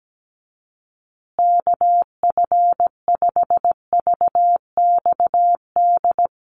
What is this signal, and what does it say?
Morse code, heard as a keyed tone, about 17 wpm, 700 Hz. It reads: KF5VXD
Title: USA Call Signs Repeated 3x 17wpm